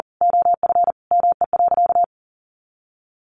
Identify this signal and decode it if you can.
Morse code, heard as a keyed tone, about 37 wpm, 700 Hz. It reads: OF ME.
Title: Sentences from Top 300 Words 40wpm